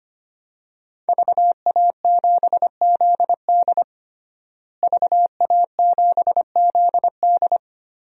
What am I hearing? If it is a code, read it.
Morse code, heard as a keyed tone, about 25 wpm, 700 Hz. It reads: VA7ZD VA7ZD
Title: DXCC-100 Call Signs - 5 Characters - Repeated 3x - 25wpm